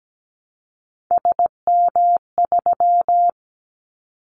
Morse code, heard as a keyed tone, about 17 wpm, 700 Hz. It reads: SM3
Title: Callsign Prefixes for Top 25 DXCC - 17wpm